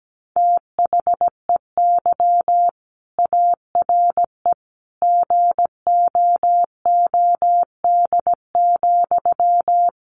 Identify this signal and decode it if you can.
Morse code, heard as a keyed tone, about 17 wpm, 700 Hz. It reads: THEY ARE GOOD,
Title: Sentences from Top 1100 Words 17wpm